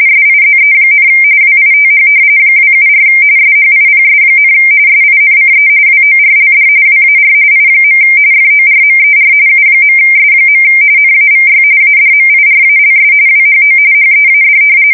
RTTY, about 45.45 baud. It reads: SIZE, WITH 274 SHIPS, SINCE WORLD WAR I. FORMER U.S. NAVY ADMIRALS WHO HEAD
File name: fsk170 / usnavy1